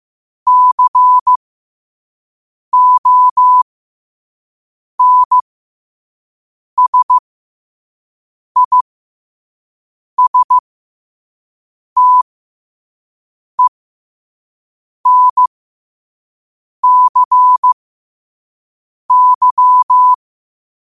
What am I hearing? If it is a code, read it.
Morse code, heard as a keyed tone, about 15 wpm, spaced as for 5 wpm, 1000 Hz. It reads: CONSISTENCY
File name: qotdfiles / 5wpm